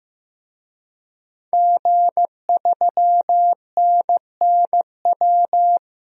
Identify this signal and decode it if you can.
Morse code, heard as a keyed tone, about 16 wpm, 700 Hz. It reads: G3NNW
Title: DXCC-100 Call Signs - 5 Characters - 15wpm